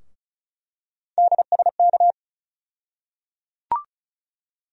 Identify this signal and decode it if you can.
Morse code, heard as a keyed tone, about 34 wpm, 700 Hz. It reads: DSK
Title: USA Call Sign Postfixes - 3 Letters 35wpm